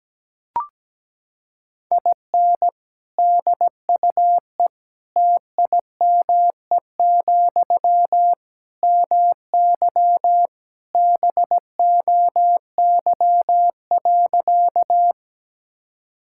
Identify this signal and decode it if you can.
Morse code, heard as a keyed tone, about 17 wpm, 700 Hz. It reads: IN DUE TIME, MY BOY.
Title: Sentences from Top 1000 Words - Review Part 6 - 17wpm